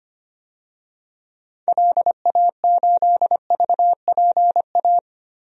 Morse code, heard as a keyed tone, about 25 wpm, 700 Hz. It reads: LA8VPA
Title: DXCC-100 Call Signs - 6 Characters - 25wpm